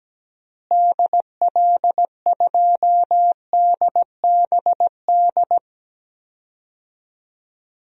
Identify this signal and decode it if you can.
Morse code, heard as a keyed tone, about 17 wpm, 700 Hz. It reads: DL2DBD
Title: DXCC-100 Call Signs - 6 Characters - Repeated 2x - 17wpm